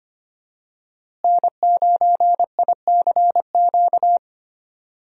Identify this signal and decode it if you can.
Morse code, heard as a keyed tone, about 25 wpm, 700 Hz. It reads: N9ICQ
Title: USA Call Signs - 5 Characters - Repeated 2x - 25wpm